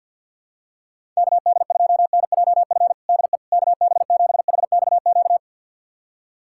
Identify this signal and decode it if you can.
Morse code, heard as a keyed tone, about 50 wpm, 700 Hz. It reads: KD2NJF DE KB7HXQ
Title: USA Callsign DE Callsign - 6 Characters - 50wpm